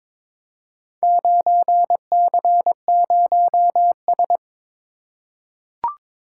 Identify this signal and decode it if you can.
Morse code, heard as a keyed tone, about 22 wpm, 700 Hz. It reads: 9C0S
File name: practice / 22wpm / Four Letter-Number Combinations - 22wpm